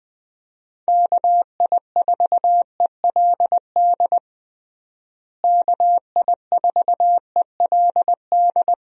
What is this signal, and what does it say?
Morse code, heard as a keyed tone, about 20 wpm, 700 Hz. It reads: KI4ELD KI4ELD
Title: USA Call Signs - 6 Characters - Repeated 3x - 20wpm